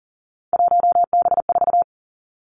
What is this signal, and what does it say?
Morse code, heard as a keyed tone, about 42 wpm, 700 Hz. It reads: 1B4
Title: Three Challenging Letter-Number Combinations 40wpm